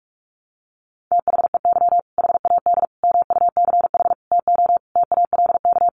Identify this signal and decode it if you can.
Morse code, heard as a keyed tone, about 46 wpm, 700 Hz. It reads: THEY HAD MUCH TO TALK
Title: Sentences from Top 400 Words 45wpm